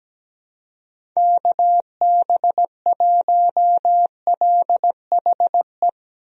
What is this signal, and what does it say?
Morse code, heard as a keyed tone, about 17 wpm, 700 Hz. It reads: KB1LHE
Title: USA Call Signs - 6 Characters - Repeated 2x - 17wpm